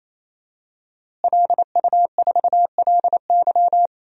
Morse code, heard as a keyed tone, about 28 wpm, 700 Hz. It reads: LU4LY
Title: DXCC-100 Call Signs - 5 Characters - Repeated 3x - 28wpm